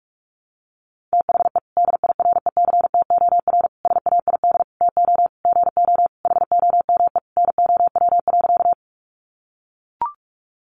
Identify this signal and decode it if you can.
Morse code, heard as a keyed tone, about 44 wpm, 700 Hz. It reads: THE DIRECTOR SAID TO GO HOME NOW.
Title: Sentences from Top 1300 Words 45wpm